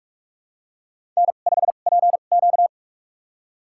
Morse code, heard as a keyed tone, about 42 wpm, spaced as for 31 wpm, 700 Hz. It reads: N5PQ
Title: USA Call Signs - 4 Characters - 2x Character Spacing - 45wpm